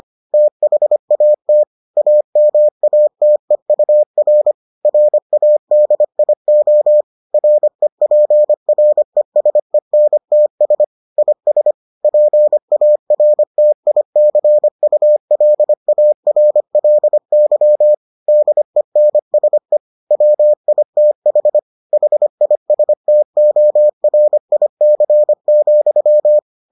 Morse code, heard as a keyed tone, about 25 wpm, 600 Hz. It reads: THAT AMATEUR RADIO REPRESENTS IS PARTICULARLY DENSE WITH HISTORIC,